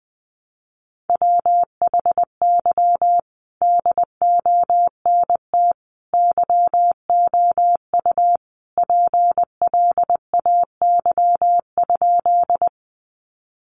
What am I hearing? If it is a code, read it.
Morse code, heard as a keyed tone, about 20 wpm, 700 Hz. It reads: WHY DONT YOU PLAY?
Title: Sentences from Top 300 Words 20wpm